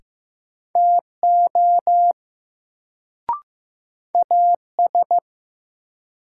Morse code, heard as a keyed tone, about 15 wpm, 700 Hz. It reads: TO AS
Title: Sentences from Top 100 Words - New Word Drill - 15wpm